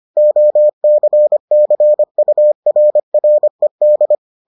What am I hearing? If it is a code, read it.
Morse code, heard as a keyed tone, about 25 wpm, 600 Hz. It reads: OCCURRED